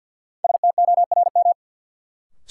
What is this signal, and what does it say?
Morse code, heard as a keyed tone, about 52 wpm, 700 Hz. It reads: STORM
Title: Sentences from Top 1500 Words - New Word Drill Alt - 50wpm